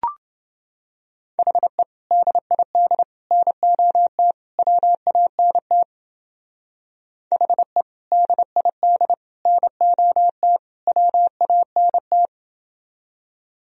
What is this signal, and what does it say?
Morse code, heard as a keyed tone, about 28 wpm, 700 Hz. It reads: HE DID NOT WANT HE DID NOT WANT